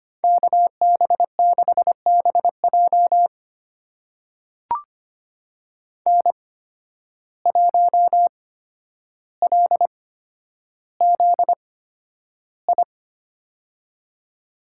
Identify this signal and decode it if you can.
Morse code, heard as a keyed tone, about 25 wpm, spaced as for 11 wpm, 700 Hz. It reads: KB6BJ N1LZI
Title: USA Call Signs - 5 Characters - 8x Character Spacing - 25wpm